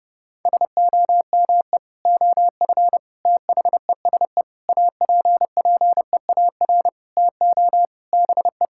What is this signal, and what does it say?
Morse code, heard as a keyed tone, about 28 wpm, 700 Hz. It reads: SOME OF THESE APPEAR TO BE
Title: Sentences from Top 1000 Words - Review Part 6 - 30wpm